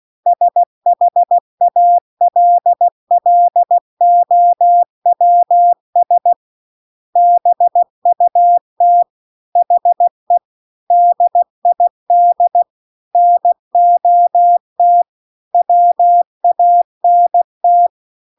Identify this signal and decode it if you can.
Morse code, heard as a keyed tone, about 16 wpm, 700 Hz. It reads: SHALLOWS BUT HE DID NOT WANT